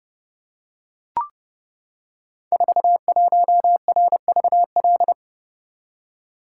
Morse code, heard as a keyed tone, about 29 wpm, 700 Hz. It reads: 41RVL
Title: Code Groups - 5 Letter:Numbers - 30wpm